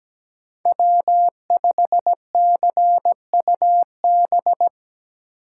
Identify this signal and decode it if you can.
Morse code, heard as a keyed tone, about 17 wpm, 700 Hz. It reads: W5CUB
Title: USA Call Signs - 5 Characters - 2x Character Spacing - 17wpm